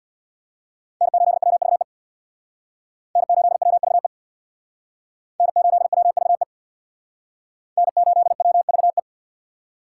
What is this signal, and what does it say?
Morse code, heard as a keyed tone, about 64 wpm, 700 Hz. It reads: N8WVE N8WVE N8WVE N8WVE